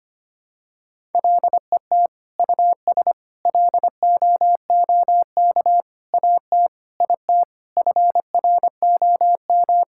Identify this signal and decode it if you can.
Morse code, heard as a keyed tone, about 25 wpm, 700 Hz. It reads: LET US LOOK AT IT FROM
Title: Sentences from Top 1000 Words - Review Part 6 - 25wpm